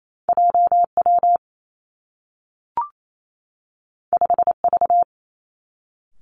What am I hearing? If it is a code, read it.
Morse code, heard as a keyed tone, about 28 wpm, 700 Hz. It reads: J W 5 V
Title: Two Challenging Letter-Number Combinations 28wpm